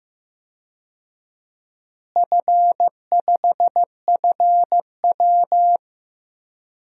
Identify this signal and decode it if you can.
Morse code, heard as a keyed tone, about 15 wpm, 700 Hz. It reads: F5FW
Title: DXCC-100 Call Signs - 4 Characters - Repeated 3x - 15wpm